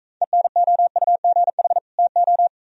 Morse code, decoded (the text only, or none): ENOUGH TO